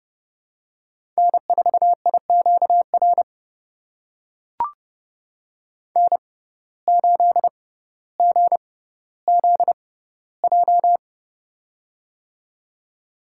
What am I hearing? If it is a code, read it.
Morse code, heard as a keyed tone, about 30 wpm, spaced as for 15 wpm, 700 Hz. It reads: N4IQR N8GZJ
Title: USA Call Signs - 5 Characters - 6x Character Spacing - 30wpm